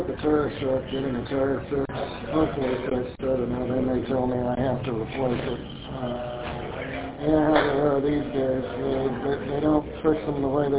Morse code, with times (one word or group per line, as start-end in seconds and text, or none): none